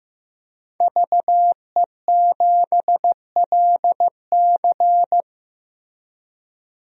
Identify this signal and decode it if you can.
Morse code, heard as a keyed tone, about 16 wpm, 700 Hz. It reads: VE7LC
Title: DXCC-100 Call Signs - 5 Characters - 15wpm